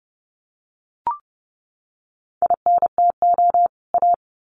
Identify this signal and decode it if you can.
Morse code, heard as a keyed tone, about 30 wpm, 700 Hz. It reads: INTO A
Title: Sets of 2 Words 30wpm